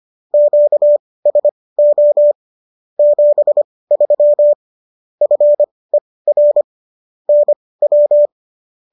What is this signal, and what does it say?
Morse code, heard as a keyed tone, about 25 wpm, spaced as for 18 wpm, 590 Hz. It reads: QSO 73 FER NW
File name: qso201 18